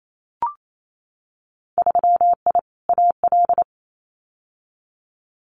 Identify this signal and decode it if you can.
Morse code, heard as a keyed tone, about 28 wpm, 700 Hz. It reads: 3I AL
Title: Winter Field Day Exchanges Encore 28wpm